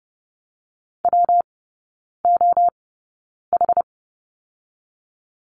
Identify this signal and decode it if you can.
Morse code, heard as a keyed tone, about 30 wpm, 700 Hz. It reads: W O H
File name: Lesson 19e - Letter w - Three Characters 30wpm